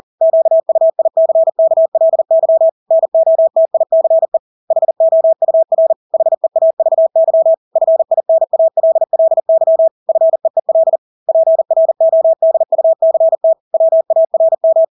Morse code, heard as a keyed tone, about 40 wpm, 650 Hz. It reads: QUICKLY NOTICE HOUR HEAVY FINALLY FEEL PRODUCT WARM
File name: Engelske ord 40-40 wpm